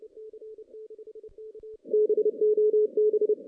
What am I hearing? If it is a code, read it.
Morse code, heard as a keyed tone, about 29 wpm, 440 Hz. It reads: BOB
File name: 2016 JAN CW NAQP / 2016 Jan naqp k5ot - 0-10